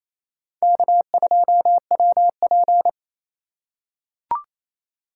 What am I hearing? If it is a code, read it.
Morse code, heard as a keyed tone, about 28 wpm, 700 Hz. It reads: K2WP
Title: USA Call Signs - 4 Characters - 3x Character Spacing - 28wpm